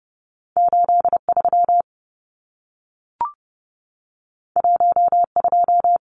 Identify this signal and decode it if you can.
Morse code, heard as a keyed tone, about 30 wpm, 700 Hz. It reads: 83 12
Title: Numbers - 2-Digit 30wpm